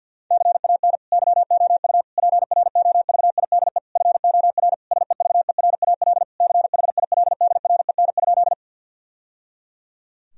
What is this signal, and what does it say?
Morse code, heard as a keyed tone, about 50 wpm, 700 Hz. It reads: CAN YOU PROVIDE FOR SEVERAL CHILDREN?
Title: Sentences from Top 1000 Words - Review Part 2 - 50wpm